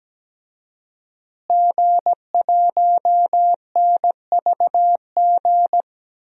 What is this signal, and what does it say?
Morse code, heard as a keyed tone, about 17 wpm, 700 Hz. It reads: G1NVG
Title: DXCC-100 Call Signs - 5 Characters - Repeated 2x - 17wpm